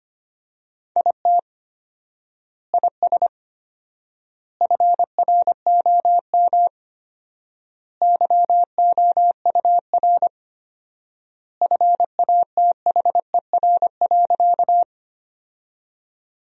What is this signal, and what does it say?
Morse code, heard as a keyed tone, about 25 wpm, 700 Hz. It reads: IT IS FROM YOUR FATHER.
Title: 4x Word-Spacing - Sentences from Top 300 Words - 25wpm